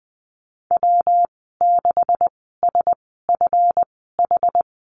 Morse code, heard as a keyed tone, about 20 wpm, spaced as for 14 wpm, 700 Hz. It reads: W6SFH
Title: 147 Lesson 24g - Letter y - Call Signs 20wpm